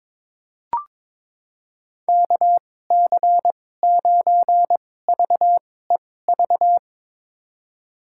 Morse code, heard as a keyed tone, about 22 wpm, spaced as for 16 wpm, 700 Hz. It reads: KC9VEV
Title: USA Call Signs - 6 Characters - 2x Character Spacing - 22wpm